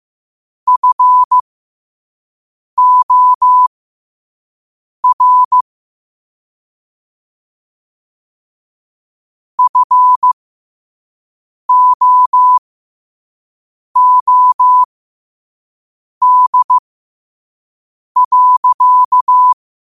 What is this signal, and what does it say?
Morse code, heard as a keyed tone, about 15 wpm, spaced as for 5 wpm, 1000 Hz. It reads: FOR FOOD.
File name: qotdfiles / 5wpm